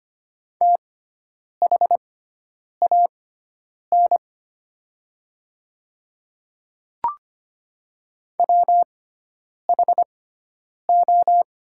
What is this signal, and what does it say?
Morse code, heard as a keyed tone, about 25 wpm, spaced as for 9 wpm, 700 Hz. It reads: THAN WHO
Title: 6x Character Spacing - Top 100 Words - 25wpm